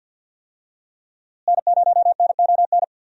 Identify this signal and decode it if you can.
Morse code, heard as a keyed tone, about 51 wpm, 700 Hz. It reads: N0NON
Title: USA Call Signs - 5 Characters - Repeated 3x - 50wpm